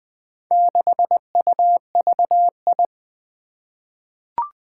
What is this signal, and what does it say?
Morse code, heard as a keyed tone, about 20 wpm, 700 Hz. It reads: 6UVI
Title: Four Challenging Letter-Number Combinations 20wpm